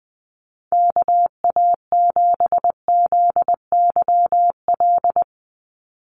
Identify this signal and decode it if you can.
Morse code, heard as a keyed tone, about 19 wpm, 700 Hz. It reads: KA7ZYL
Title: USA Call Signs 20wpm